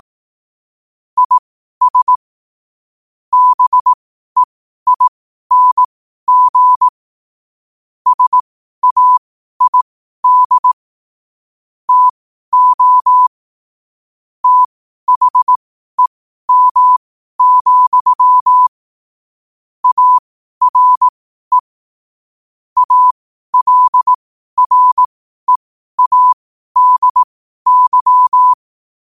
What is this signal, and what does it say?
Morse code, heard as a keyed tone, about 18 wpm, spaced as for 12 wpm, 1000 Hz. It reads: IS BEING SAID TO THEM, ARE ALREADY
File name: qotdfiles / 12wpm